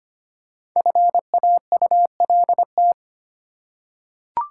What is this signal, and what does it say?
Morse code, heard as a keyed tone, about 25 wpm, 700 Hz. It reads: FAULT
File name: 5 Letter Words - First 3 Letters Given - 25wpm